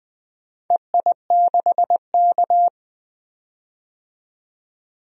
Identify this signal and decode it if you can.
Morse code, heard as a keyed tone, about 20 wpm, 700 Hz. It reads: EI6K
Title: DXCC-100 Call Signs - 4 Characters - 20wpm